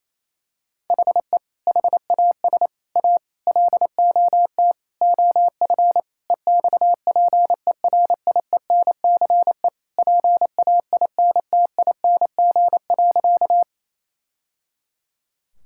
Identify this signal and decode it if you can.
Morse code, heard as a keyed tone, about 28 wpm, 700 Hz. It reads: HE HAS A LOT OF EXPERIENCE PAINTING.
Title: Sentences from Top 1000 Words - Review Part 6 - 28wpm